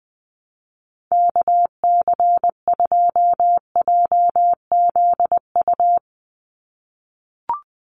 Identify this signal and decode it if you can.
Morse code, heard as a keyed tone, about 20 wpm, 700 Hz. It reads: KC2JZU
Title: USA Call Signs 20wpm